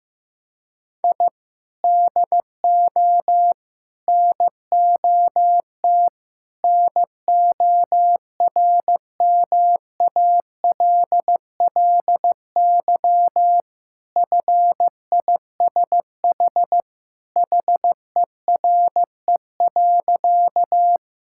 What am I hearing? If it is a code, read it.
Morse code, heard as a keyed tone, about 15 wpm, 700 Hz. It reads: I DO NOT NORMALLY FISH HERE.
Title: Sentences from Top 1000 Words - Review Part 6 - 15wpm